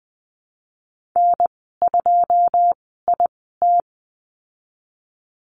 Morse code, heard as a keyed tone, about 19 wpm, spaced as for 14 wpm, 700 Hz. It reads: N2IT
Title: 173 Lesson 28g - Number 9 - Call Signs 20wpm